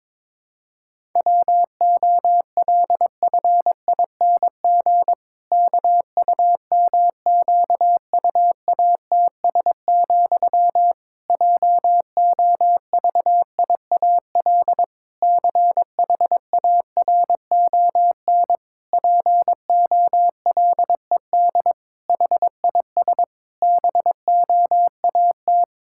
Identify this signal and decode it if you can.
Morse code, heard as a keyed tone, about 22 wpm, 700 Hz. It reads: WOLFING KUMQUATS, JOVIAL CHARON POLED HIS BOAT
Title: Pangram Challenge v2 - 22wpm